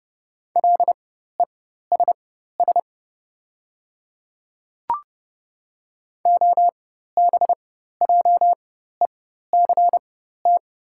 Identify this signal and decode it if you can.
Morse code, heard as a keyed tone, about 30 wpm, spaced as for 14 wpm, 700 Hz. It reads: LESS OBJECT
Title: 4x Character Spacing - Top 500 Words - 30wpm